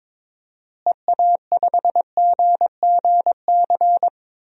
Morse code, heard as a keyed tone, about 22 wpm, 700 Hz. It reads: EA5GGC
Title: DXCC-100 Call Signs - 6 Characters - Repeated 2x - 22wpm